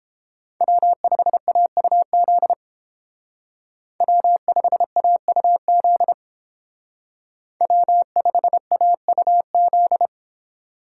Morse code, heard as a keyed tone, about 28 wpm, 700 Hz. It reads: W5AUZ W5AUZ W5AUZ